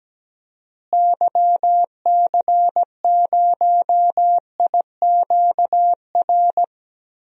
Morse code, heard as a keyed tone, about 17 wpm, 700 Hz. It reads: YC0IQR
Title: DXCC-100 Call Signs - 6 Characters - Repeated 2x - 17wpm